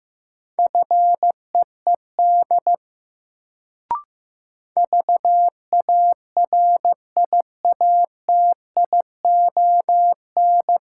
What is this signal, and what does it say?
Morse code, heard as a keyed tone, about 15 wpm, 700 Hz. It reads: FEED VARIATION
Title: Sentences from Top 1600 Words - New Word Drill Alt - 15wpm